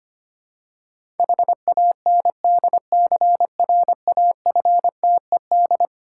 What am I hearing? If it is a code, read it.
Morse code, heard as a keyed tone, about 25 wpm, 700 Hz. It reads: HANDCRAFTED
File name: practice / 25wpm / Difficult Words with Hints - 25wpm